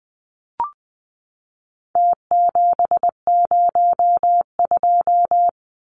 Morse code, heard as a keyed tone, about 20 wpm, 700 Hz. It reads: T702
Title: Numbers - Contest Style - 4-Digit 20wpm